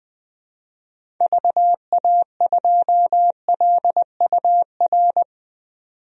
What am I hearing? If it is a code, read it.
Morse code, heard as a keyed tone, about 21 wpm, 700 Hz. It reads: VA2LUR
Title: DXCC-100 Call Signs - 6 Characters - 20wpm